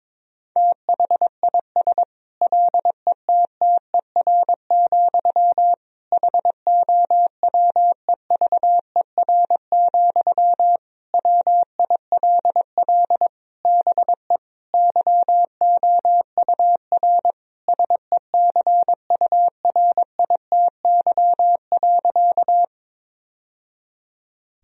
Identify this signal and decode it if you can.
Morse code, heard as a keyed tone, about 21 wpm, 700 Hz. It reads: THIS LETTER, HOWEVER, WILL BE YOUR SECURITY.
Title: Sentences from Top 1000 Words - Review Part 4 - 22wpm